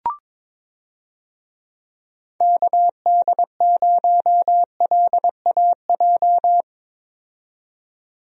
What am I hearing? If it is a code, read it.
Morse code, heard as a keyed tone, about 22 wpm, 700 Hz. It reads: KD0LAJ
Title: USA Call Signs - 6 Characters - 22wpm